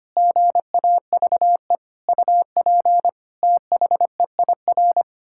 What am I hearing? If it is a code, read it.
Morse code, heard as a keyed tone, about 25 wpm, 700 Hz. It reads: GAVE UP THEIR
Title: Pangram Challenge v2 - 25wpm